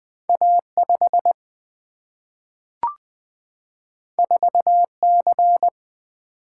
Morse code, heard as a keyed tone, about 20 wpm, 700 Hz. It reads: A5 4C
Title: Single-Letter Single-Number 20wpm